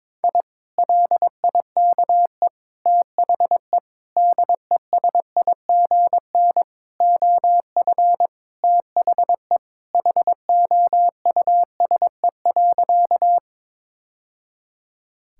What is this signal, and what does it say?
Morse code, heard as a keyed tone, about 22 wpm, 700 Hz. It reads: I LIKE THE DESIGN OF THE HOUSE.
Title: Sentences from Top 1000 Words - Review Part 5 - 22wpm